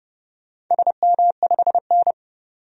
Morse code, heard as a keyed tone, about 31 wpm, 700 Hz. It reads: SM5N
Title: DXCC-100 Call Signs - 4 Characters - 30wpm